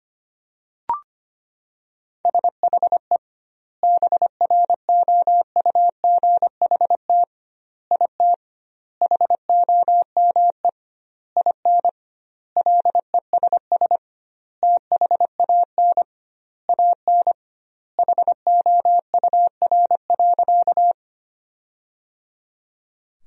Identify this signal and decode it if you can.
Morse code, heard as a keyed tone, about 25 wpm, 700 Hz. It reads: SHE BROUGHT IT HOME IN LESS THAN AN HOUR.
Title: 2x Word-Spacing - Sentences from Top 500 Words - 25wpm